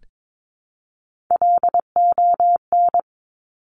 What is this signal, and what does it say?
Morse code, heard as a keyed tone, about 22 wpm, 700 Hz. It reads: LON
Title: Top Three Letter Combinations in Words 22wpm